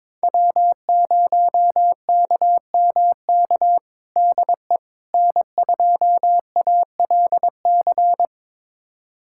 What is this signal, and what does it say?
Morse code, heard as a keyed tone, about 21 wpm, 700 Hz. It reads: W0KMK DE N2ALC
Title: USA Callsign DE Callsign - 5 Characters - 22wpm